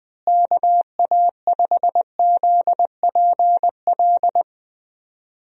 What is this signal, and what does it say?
Morse code, heard as a keyed tone, about 20 wpm, 700 Hz. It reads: KA5ZPL
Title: USA Call Signs - 6 Characters - Repeated 2x - 20wpm